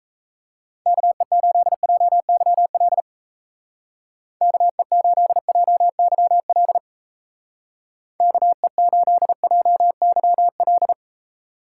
Morse code, heard as a keyed tone, about 34 wpm, 700 Hz. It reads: KE8JYL KE8JYL KE8JYL
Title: Speed Racing - USA Call Signs 28wpm